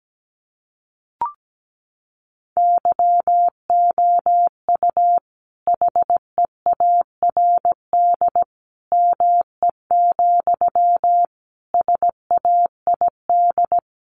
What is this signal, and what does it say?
Morse code, heard as a keyed tone, about 17 wpm, 700 Hz. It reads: YOU HEARD ME, SAID
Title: A Sound Of Thunder 17wpm